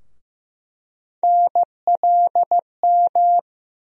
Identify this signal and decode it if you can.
Morse code, heard as a keyed tone, about 15 wpm, 700 Hz. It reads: NLM
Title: USA Call Sign Postfixes - 3 Letters 15wpm